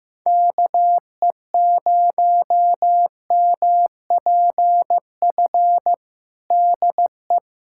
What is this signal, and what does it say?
Morse code, heard as a keyed tone, about 15 wpm, 700 Hz. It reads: KE0MPF DE
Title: Practice QSO - Exchange 4 - Ending - 15wpm